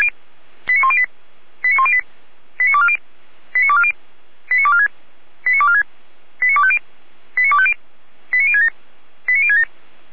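Five-tone selective call (ZVEI), heard as quick runs of five tones, tones of about 70 ms. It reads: 89108 89108 89240 89240 89246 89246 89250 89250 89067 89067
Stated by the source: ZVEI five-tone selective call received on SDR, logged by feuerwehr-orlamuende (fire brigade Orlamünde).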